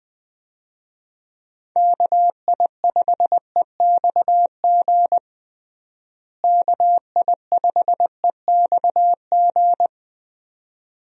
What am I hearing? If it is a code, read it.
Morse code, heard as a keyed tone, about 21 wpm, 700 Hz. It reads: KI5EXG KI5EXG
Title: USA Call Signs - 6 Characters - Repeated 2x - 20wpm